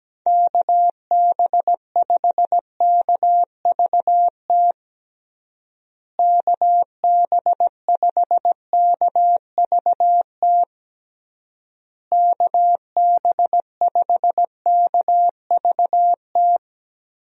Morse code, heard as a keyed tone, about 17 wpm, 700 Hz. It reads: KB5KVT KB5KVT KB5KVT
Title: USA Call Signs - 6 Characters - Repeated 3x - 17wpm